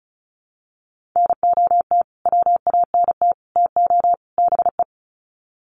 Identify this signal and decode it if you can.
Morse code, heard as a keyed tone, about 34 wpm, 700 Hz. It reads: NOT WANT TO BE
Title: Sets of 4 Words 35wpm